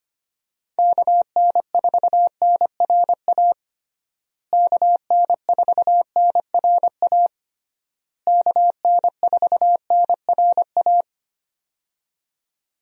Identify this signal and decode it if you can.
Morse code, heard as a keyed tone, about 25 wpm, 700 Hz. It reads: KN4NRA KN4NRA KN4NRA
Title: USA Call Signs - 6 Characters - Repeated 3x - 25wpm